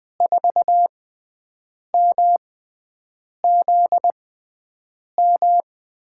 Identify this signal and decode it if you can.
Morse code, heard as a keyed tone, about 20 wpm, spaced as for 7 wpm, 700 Hz. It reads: 4MZM
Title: USA Call Signs - 6 Characters - 6x Character Spacing - 20wpm